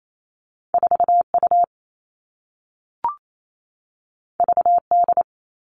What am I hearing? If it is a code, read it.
Morse code, heard as a keyed tone, about 28 wpm, 700 Hz. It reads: 4U VD
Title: All Cut Numbers - 2-Digit 28wpm